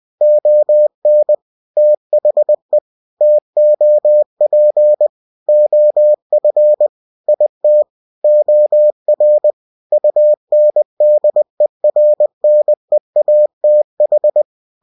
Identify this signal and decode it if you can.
Morse code, heard as a keyed tone, about 20 wpm, 600 Hz. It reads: ON THE TOP OF IT OR UNDERNEATH
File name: cw-books / winnie-the-pooh / winnie-20wpm-page-0099